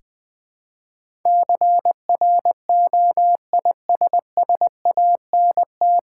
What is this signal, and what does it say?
Morse code, heard as a keyed tone, about 20 wpm, 700 Hz. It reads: CROISSANT
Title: Easy Words with Hints - 20wpm